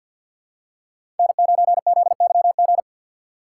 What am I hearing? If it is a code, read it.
Morse code, heard as a keyed tone, about 52 wpm, 700 Hz. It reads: N9ZYG
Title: USA Call Signs - 5 Characters - 2x Character Spacing - 50wpm